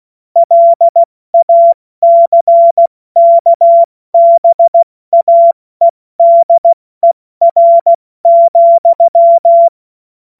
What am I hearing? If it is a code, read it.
Morse code, heard as a keyed tone, about 16 wpm, 680 Hz. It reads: LACKBAEDER,
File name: Witz-14-Oktober-2025-70-F80